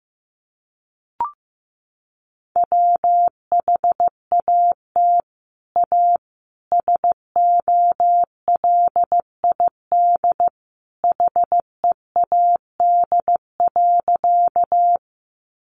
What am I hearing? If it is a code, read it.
Morse code, heard as a keyed tone, about 15 wpm, 700 Hz. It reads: WHAT A SOLID HEAD.
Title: Sentences from Top 1500 Words 15wpm